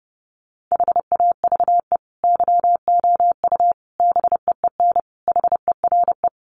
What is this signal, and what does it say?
Morse code, heard as a keyed tone, about 30 wpm, 700 Hz. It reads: HAVE YOU BEEN HERE
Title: Sentences from Top 200 Words 30wpm